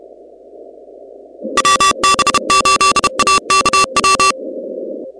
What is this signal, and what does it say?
Morse code, heard as a keyed tone, about 31 wpm, 1160 Hz, with noise. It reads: WB8AKW
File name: contests / 2019-cw-wpx-kt5j-k5pi- -2330 0-16